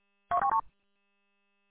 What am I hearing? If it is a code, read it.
There is no code.